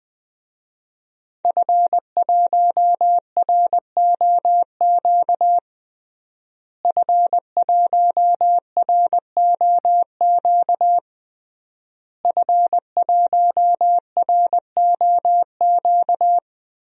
Morse code, heard as a keyed tone, about 20 wpm, 700 Hz. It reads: F1ROQ F1ROQ F1ROQ
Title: DXCC-100 Call Signs - 5 Characters - Repeated 3x - 20wpm